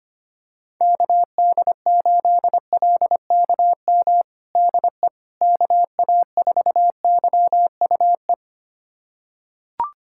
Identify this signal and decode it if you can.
Morse code, heard as a keyed tone, about 25 wpm, 700 Hz. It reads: KD8LKM DE KA4YUE
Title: USA Callsign DE Callsign - 6 Characters - 25wpm